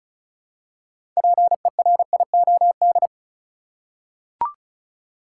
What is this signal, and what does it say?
Morse code, heard as a keyed tone, about 36 wpm, 700 Hz. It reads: PERIOD
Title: Six Letter Words - 35wpm